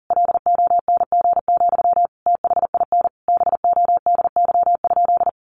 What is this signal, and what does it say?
Morse code, heard as a keyed tone, about 40 wpm, 700 Hz. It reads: LONG, THIN BODY?
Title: Sentences from Top 1200 Words 40wpm